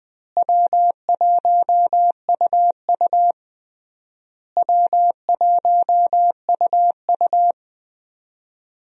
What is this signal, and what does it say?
Morse code, heard as a keyed tone, about 20 wpm, 700 Hz. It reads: W1UU W1UU
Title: USA Call Signs - 4 Characters - Repeated 3x - 20wpm